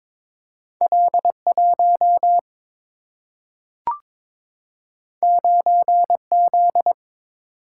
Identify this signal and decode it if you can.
Morse code, heard as a keyed tone, about 22 wpm, 700 Hz. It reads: L 1 9 Z
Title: Single-Letter Single-Number 22wpm